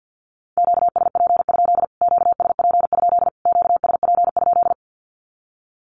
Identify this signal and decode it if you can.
Morse code, heard as a keyed tone, about 48 wpm, 700 Hz. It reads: QSP? QSP? QSP?
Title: Q-Codes Repeated 3x 50wpm